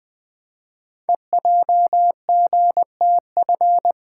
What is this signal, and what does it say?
Morse code, heard as a keyed tone, about 20 wpm, 700 Hz. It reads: EJGTF
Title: Code Groups - 5 Letters - 20wpm